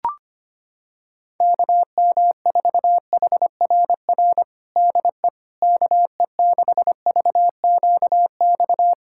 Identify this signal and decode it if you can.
Morse code, heard as a keyed tone, about 25 wpm, 700 Hz. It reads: KM4HRR DE KE6VQX
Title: USA Callsign DE Callsign 25wpm